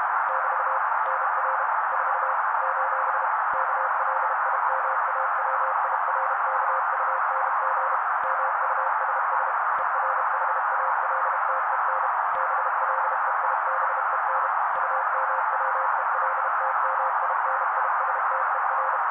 Morse code, heard as a keyed tone, about 32 wpm, 530 Hz. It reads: <BT> NR 4 : KLIMAWIRKUNG MUSS ERHALTEN BLEIBEN UMWELTMINISTER
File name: daily / 31wpm noise